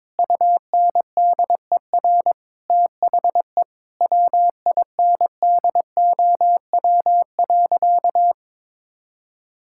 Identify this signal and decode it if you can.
Morse code, heard as a keyed tone, about 22 wpm, 700 Hz. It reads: UNDER THE WINDOW.
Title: Sentences from Top 1600 Words 22wpm